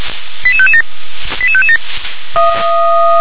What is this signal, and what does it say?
Five-tone selective call (ZVEI), heard as quick runs of five tones, tones of about 70 ms. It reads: 88557 88557
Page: SDR/ZVEI